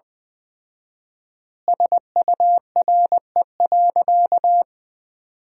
Morse code, heard as a keyed tone, about 20 wpm, 700 Hz. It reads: SURE.
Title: 4x Word-Spacing - Sentences from Top 500 Words - 20wpm